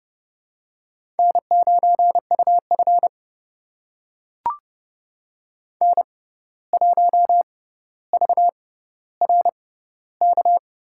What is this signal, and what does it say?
Morse code, heard as a keyed tone, about 29 wpm, spaced as for 14 wpm, 700 Hz. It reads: N9UF N1VRK
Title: USA Call Signs - 6x Character Spacing - 30wpm